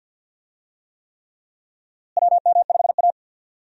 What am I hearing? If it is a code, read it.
Morse code, heard as a keyed tone, about 52 wpm, 700 Hz. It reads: WM5A
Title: USA Call Signs - 4 Characters - Repeated 3x - 50wpm